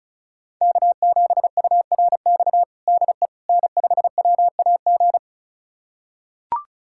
Morse code, heard as a keyed tone, about 36 wpm, 700 Hz. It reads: K7URX DE N5WAG
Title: USA Callsign DE Callsign - 5 Characters - 35wpm